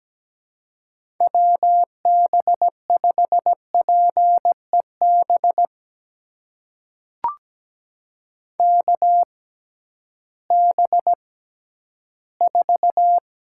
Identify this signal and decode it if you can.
Morse code, heard as a keyed tone, about 17 wpm, spaced as for 11 wpm, 700 Hz. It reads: WB5PEB KB4
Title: USA Call Signs - 6 Characters - 6x Character Spacing - 17wpm